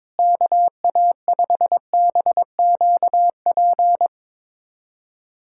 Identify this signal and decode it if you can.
Morse code, heard as a keyed tone, about 22 wpm, 700 Hz. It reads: KA5BQP
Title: USA Callsign DE Callsign - 6 Characters - 22wpm